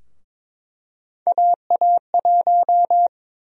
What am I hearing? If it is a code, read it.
Morse code, heard as a keyed tone, about 21 wpm, 700 Hz. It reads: AA1
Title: USA Call Sign Prefixes - 22wpm